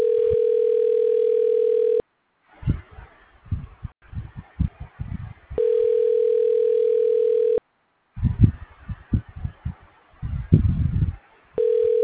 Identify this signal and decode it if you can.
Ringback tone (North America), 440 + 480 Hz.